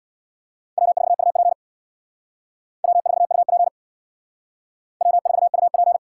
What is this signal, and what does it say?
Morse code, heard as a keyed tone, about 63 wpm, 700 Hz. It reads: W4UP W4UP W4UP